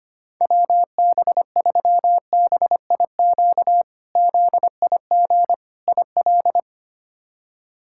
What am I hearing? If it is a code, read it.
Morse code, heard as a keyed tone, about 25 wpm, 700 Hz. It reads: WB3BIQ ZIG IL
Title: K1USN SST QSO Practice - 25wpm